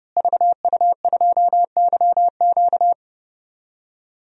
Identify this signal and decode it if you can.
Morse code, heard as a keyed tone, about 30 wpm, 700 Hz. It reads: VU2YQ
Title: DXCC-100 Call Signs - 5 Characters - Repeated 3x - 30wpm